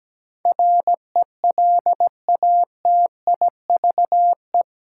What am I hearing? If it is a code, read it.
Morse code, heard as a keyed tone, about 17 wpm, 700 Hz. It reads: RELATIVE
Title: Sentences from Top 1400 Words - New Word Drill Alt - 17wpm